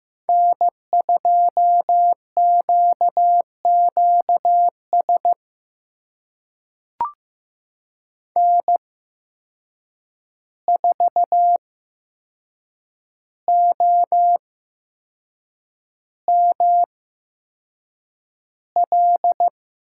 Morse code, heard as a keyed tone, about 15 wpm, spaced as for 6 wpm, 700 Hz. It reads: N2QQS N4OML
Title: USA Call Signs - 5 Characters - 8x Character Spacing - 15wpm